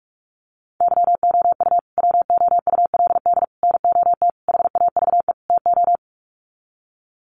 Morse code, heard as a keyed tone, about 44 wpm, 700 Hz. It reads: YOU WOULD NOT HAVE TO
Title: Sets of 5 Words Encore 45wpm